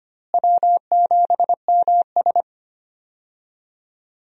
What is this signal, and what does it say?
Morse code, heard as a keyed tone, about 25 wpm, 700 Hz. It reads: W7MS
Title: USA Call Signs - 4 Characters - Repeated 3x - 25wpm